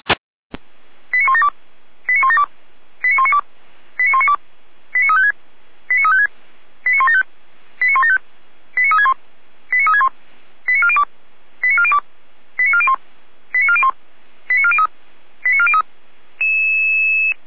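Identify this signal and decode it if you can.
Five-tone selective call (ZVEI), heard as quick runs of five tones, tones of about 70 ms. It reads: 89172 89172 89192 89192 89356 89356 89175 89175 89361 89361 89402 89402 89401 89401 89403 89403
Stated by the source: ZVEI five-tone selective call received by SDR.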